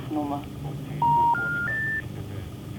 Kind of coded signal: telephone call progress tone